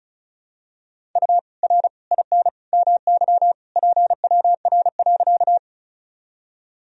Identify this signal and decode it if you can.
Morse code, heard as a keyed tone, about 36 wpm, 700 Hz. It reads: U R IN MY PWR.
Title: Sentences from Top 500 Words and Abbreviations 35wpm